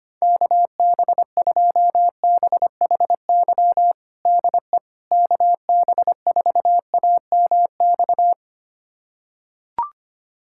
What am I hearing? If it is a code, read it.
Morse code, heard as a keyed tone, about 25 wpm, 700 Hz. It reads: KB2BHY DE KB4AMX
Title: USA Callsign DE Callsign - 6 Characters - 25wpm